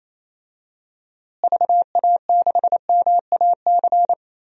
Morse code, heard as a keyed tone, about 28 wpm, 700 Hz. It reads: VA6MAC
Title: DXCC-100 Call Signs - 6 Characters - Repeated 3x - 28wpm